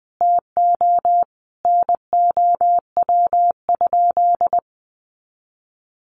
Morse code, heard as a keyed tone, about 19 wpm, 700 Hz. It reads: TO NOW?
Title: Sentences from Top 200 Words 20wpm